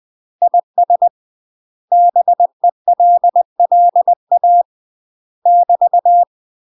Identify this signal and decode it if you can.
Morse code, heard as a keyed tone, about 20 wpm, 700 Hz. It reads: IS BELLA <BT>
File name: UK MASTER QSOs TRAINING / 20WPM / UK MASTER QSOs 0045